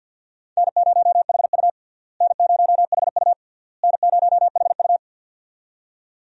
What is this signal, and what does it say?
Morse code, heard as a keyed tone, about 52 wpm, 700 Hz. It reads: N0HU N0HU N0HU
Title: USA Call Signs - 4 Characters - Repeated 3x - 50wpm